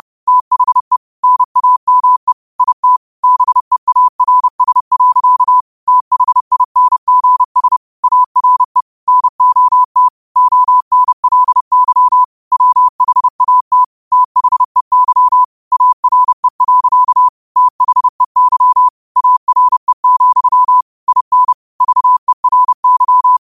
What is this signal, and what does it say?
Morse code, heard as a keyed tone, about 30 wpm, 1000 Hz. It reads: THE NAME IT BEARS. THINGS ARE NOT ONLY WHAT THEY ARE. THEY ARE, IN VERY
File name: qotdfiles / 30wpm